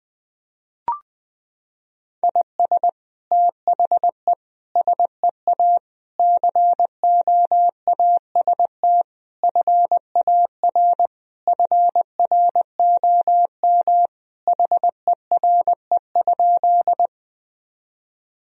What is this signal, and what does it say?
Morse code, heard as a keyed tone, about 20 wpm, 700 Hz. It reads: IS THE SEA COAST FAR FROM HERE?
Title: Sentences from Top 1000 Words - Review Part 4 - 20wpm